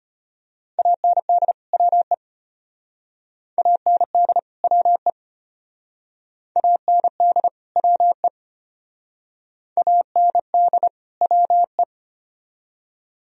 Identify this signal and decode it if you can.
Morse code, heard as a keyed tone, about 30 wpm, 700 Hz. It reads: AND WE AND WE AND WE AND WE
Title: Speed-Racing - Sets of 2 Words 25wpm